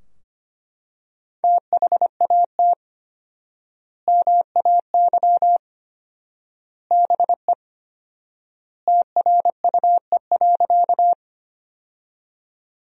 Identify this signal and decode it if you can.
Morse code, heard as a keyed tone, about 25 wpm, 700 Hz. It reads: THAT MAY BE TRUE.
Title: 4x Word-Spacing - Sentences from Top 400 Words - 25wpm